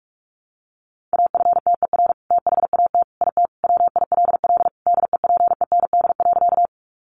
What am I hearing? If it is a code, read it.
Morse code, heard as a keyed tone, about 44 wpm, 700 Hz. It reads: AFTER THAT IT WILL DEPEND.
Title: Sentences from Top 1300 Words 45wpm